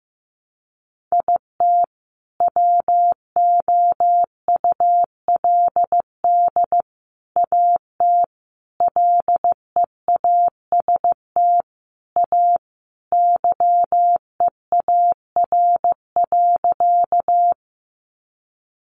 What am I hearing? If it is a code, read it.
Morse code, heard as a keyed tone, about 15 wpm, 700 Hz. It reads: IT WOULD AT LEAST A YEAR.
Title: Sentences from Top 700 Words 15wpm